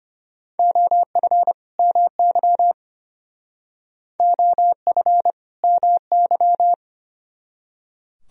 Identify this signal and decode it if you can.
Morse code, heard as a keyed tone, about 31 wpm, 700 Hz. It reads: OF MY OF MY